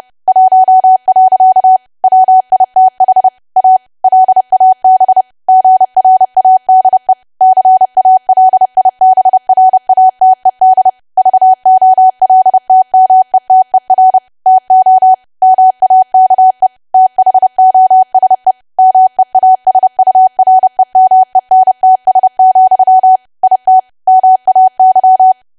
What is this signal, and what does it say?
Morse code, heard as a keyed tone, about 30 wpm, 750 Hz. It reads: 1. WITH A LAB GRADE CALIBRATED VOLTMETER TO MAKE THOSE MEASUREMENTS, IT MAY
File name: cwa / W1AW-Level3 / 30WPM